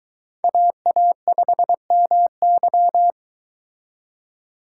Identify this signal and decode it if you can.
Morse code, heard as a keyed tone, about 23 wpm, 700 Hz. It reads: AA5MY